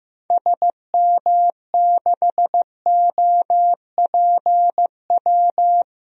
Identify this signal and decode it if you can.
Morse code, heard as a keyed tone, about 16 wpm, 700 Hz. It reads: SM6OPW
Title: DXCC-100 Call Signs - 6 Characters - 15wpm